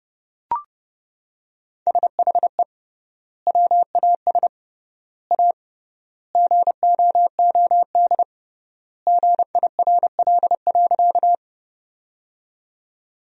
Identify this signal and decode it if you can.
Morse code, heard as a keyed tone, about 30 wpm, 700 Hz. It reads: SHE WAS A GOOD GIRL.
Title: 3x Word-Spacing - Sentences from Top 400 Words - 30wpm